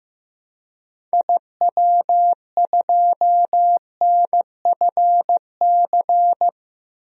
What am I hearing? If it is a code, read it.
Morse code, heard as a keyed tone, about 15 wpm, 700 Hz. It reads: IW2NFC
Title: DXCC-100 Call Signs - 6 Characters - 15wpm